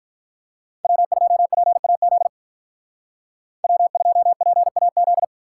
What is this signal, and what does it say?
Morse code, heard as a keyed tone, about 44 wpm, 700 Hz. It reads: W2PAZ W2PAZ